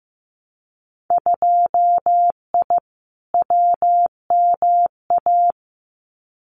Morse code, heard as a keyed tone, about 15 wpm, 700 Hz. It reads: 2I WMA
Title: Winter Field Day Exchanges Encore 15wpm